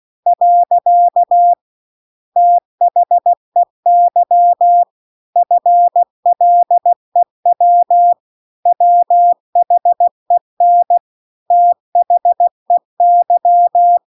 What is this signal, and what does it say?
Morse code, heard as a keyed tone, about 16 wpm, 700 Hz. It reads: . THEY FLEW WHEN THEY